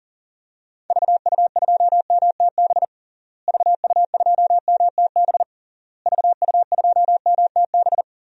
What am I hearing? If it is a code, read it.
Morse code, heard as a keyed tone, about 40 wpm, 700 Hz. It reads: VU2MTB VU2MTB VU2MTB
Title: DXCC-100 Call Signs - 6 Characters - Repeated 3x - 40wpm